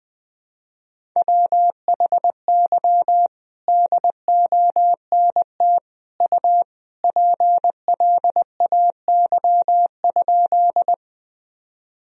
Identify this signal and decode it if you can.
Morse code, heard as a keyed tone, about 20 wpm, 700 Hz. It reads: WHY DONT U PLAY?
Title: Sentences from Top 300 Words and Abbreviations 20wpm